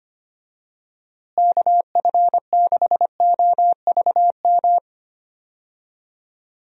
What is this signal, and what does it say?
Morse code, heard as a keyed tone, about 25 wpm, 700 Hz. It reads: KF6OVM
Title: USA Call Signs - 6 Characters - Repeated 2x - 25wpm